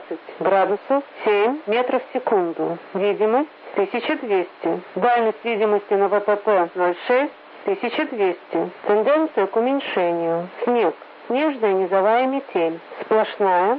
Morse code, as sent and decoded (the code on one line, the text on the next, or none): none